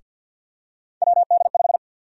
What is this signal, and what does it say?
Morse code, heard as a keyed tone, about 52 wpm, 700 Hz. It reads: WD5
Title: USA Call Sign Prefixes - 50wpm